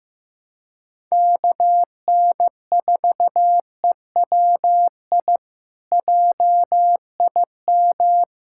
Morse code, heard as a keyed tone, about 15 wpm, 700 Hz. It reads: KN4EWI JIM
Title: K1USN SST QSO Practice - 15wpm